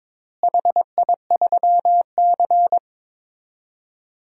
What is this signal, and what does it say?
Morse code, heard as a keyed tone, about 22 wpm, 700 Hz. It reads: HI3C
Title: DXCC-100 Call Signs - 4 Characters - Repeated 2x - 22wpm